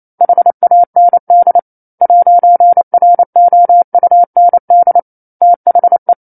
Morse code, heard as a keyed tone, about 28 wpm, 700 Hz. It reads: HAND 'ROUND THE
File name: 09-Chapter-09 28 WPM